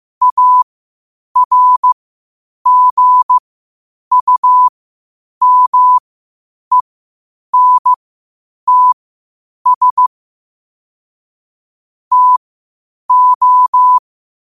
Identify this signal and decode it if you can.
Morse code, heard as a keyed tone, about 15 wpm, spaced as for 8 wpm, 1000 Hz. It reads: ARGUMENTS TO
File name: qotdfiles / 8wpm